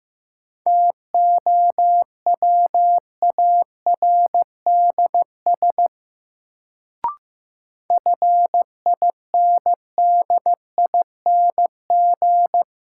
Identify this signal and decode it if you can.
Morse code, heard as a keyed tone, about 15 wpm, 700 Hz. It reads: TOWARDS FINDING
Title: Sentences from Top 1100 Words - New Word Drill Alt - 15wpm